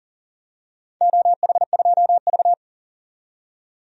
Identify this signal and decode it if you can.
Morse code, heard as a keyed tone, about 40 wpm, 700 Hz. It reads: OH2V
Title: DXCC-100 Call Signs - 4 Characters - 40wpm